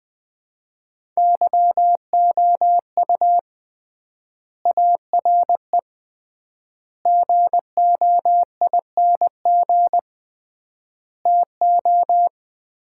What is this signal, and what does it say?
Morse code, heard as a keyed tone, about 20 wpm, 700 Hz. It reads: YOU ARE GOING TO
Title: 3x Word-Spacing - Sets of 4 Words - 20wpm